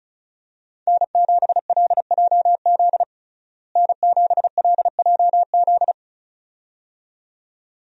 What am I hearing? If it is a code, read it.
Morse code, heard as a keyed tone, about 34 wpm, 700 Hz. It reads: N7LJZ N7LJZ
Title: USA Call Signs - 5 Characters - Repeated 2x - 35wpm